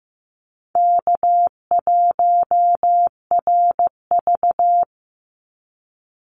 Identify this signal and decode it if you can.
Morse code, heard as a keyed tone, about 15 wpm, 700 Hz. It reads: K1RV
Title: Easy Call Signs Repeated 2x - 15wpm